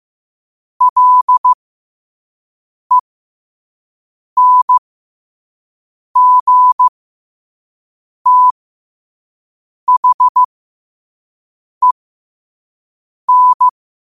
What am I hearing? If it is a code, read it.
Morse code, heard as a keyed tone, about 15 wpm, spaced as for 5 wpm, 1000 Hz. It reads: LENGTHEN
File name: qotdfiles / 5wpm